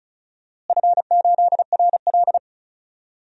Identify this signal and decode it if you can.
Morse code, heard as a keyed tone, about 36 wpm, 700 Hz. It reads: F8RL
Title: DXCC-100 Call Signs - 4 Characters - Repeated 2x - 35wpm